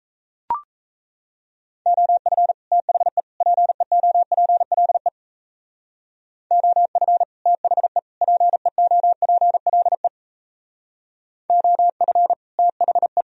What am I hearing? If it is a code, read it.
Morse code, heard as a keyed tone, about 35 wpm, 700 Hz. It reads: OF THE PEOPLE OF THE PEOPLE OF THE
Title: Speed-Racing - Sets of 3 Words 28wpm